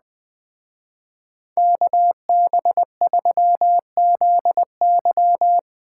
Morse code, heard as a keyed tone, about 21 wpm, 700 Hz. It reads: KB3ZY